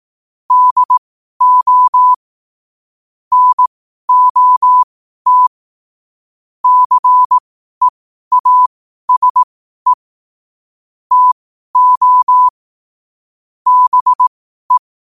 Morse code, heard as a keyed tone, about 18 wpm, spaced as for 12 wpm, 1000 Hz. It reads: DO NOT CEASE TO BE
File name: 12wpm